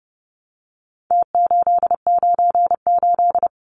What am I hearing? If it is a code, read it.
Morse code, heard as a keyed tone, about 30 wpm, 700 Hz. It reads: T898
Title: Numbers - Contest Style - 4-Digit 30wpm